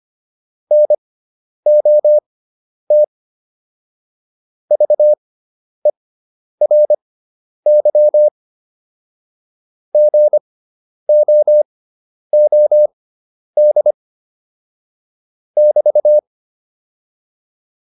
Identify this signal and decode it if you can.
Morse code, heard as a keyed tone, about 25 wpm, spaced as for 10 wpm, 600 Hz. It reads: NOT VERY GOOD <BT>